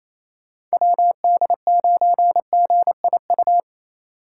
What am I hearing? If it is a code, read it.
Morse code, heard as a keyed tone, about 33 wpm, 700 Hz. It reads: WD9GIU
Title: Speed Racing - USA 2x3 Call Signs 28wpm